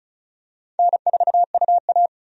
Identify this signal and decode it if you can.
Morse code, heard as a keyed tone, about 36 wpm, 700 Hz. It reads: N4UA
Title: USA Call Signs - 4 Characters - 35wpm